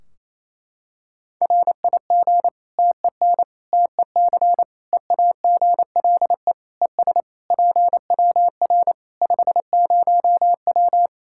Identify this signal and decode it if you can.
Morse code, heard as a keyed tone, about 28 wpm, 700 Hz. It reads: RIG TEN TEC EAGLE ES PWR 50W
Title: Practice QSO - Exchange 2 - Info - 28wpm